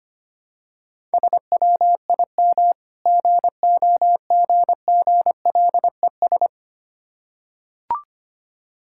Morse code, heard as a keyed tone, about 25 wpm, 700 Hz. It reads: SWIM GOGGLES
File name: practice / 25wpm / Difficult Words with Hints - 25wpm